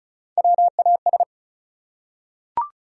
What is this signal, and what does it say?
Morse code, heard as a keyed tone, about 36 wpm, 700 Hz. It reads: WAS
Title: Top 100 Words 35wpm